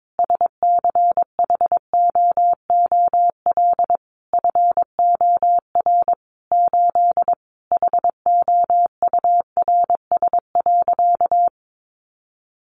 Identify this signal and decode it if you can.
Morse code, heard as a keyed tone, about 22 wpm, 700 Hz. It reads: SCHOOL FOR 8 HOURS.
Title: Additional Easy Phrases 22wpm